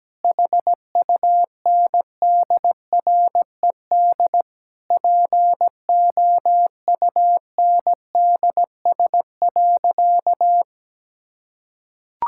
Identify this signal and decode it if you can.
Morse code, heard as a keyed tone, about 17 wpm, 700 Hz. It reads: HUNDRED POUNDS.
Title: Sentences from Top 1000 Words - Review Part 6 - 17wpm